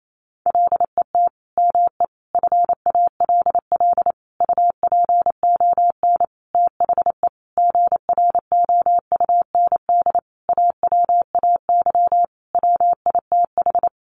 Morse code, heard as a keyed tone, about 28 wpm, 700 Hz. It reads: LET ME FALL UPON THE GROUND AWAY WITH
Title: Sentences from Top 1100 Words 28wpm